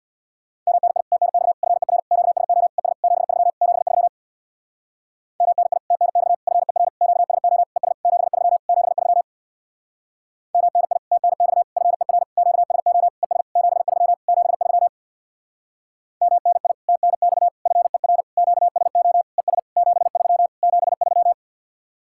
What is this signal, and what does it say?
Morse code, heard as a keyed tone, about 67 wpm, 700 Hz. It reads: MNI TNX FER QSO ES 73 73 MNI TNX FER QSO ES 73 73 MNI TNX FER QSO ES 73 73 MNI TNX FER QSO ES 73 73